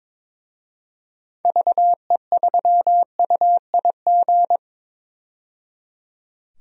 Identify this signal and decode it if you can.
Morse code, heard as a keyed tone, about 22 wpm, 700 Hz. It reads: VE3UIG
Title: DXCC-100 Call Signs - 6 Characters - 22wpm